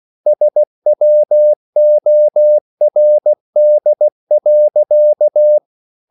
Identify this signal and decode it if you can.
Morse code, heard as a keyed tone, about 16 wpm, 600 Hz. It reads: SWORD.
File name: CW / Practice / EnglishSayings16